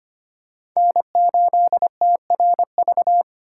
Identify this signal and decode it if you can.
Morse code, heard as a keyed tone, about 25 wpm, 700 Hz. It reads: N8TRV
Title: USA Call Signs - 4x Character Spacing - 25wpm